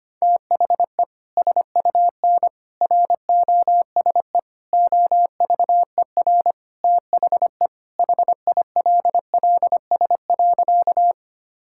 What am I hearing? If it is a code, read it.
Morse code, heard as a keyed tone, about 25 wpm, 700 Hz. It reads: THE SUN ROSE OVER THE HILLS.
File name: practice / 25wpm / Sentences with 2-Word Phrases - 25wpm